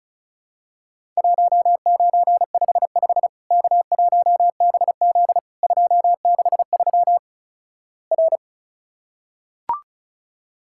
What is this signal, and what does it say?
Morse code, heard as a keyed tone, about 36 wpm, 700 Hz. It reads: 1955 K1BZ 263 R
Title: Worked All Europe DX Contest - QTC Exchange - 35wpm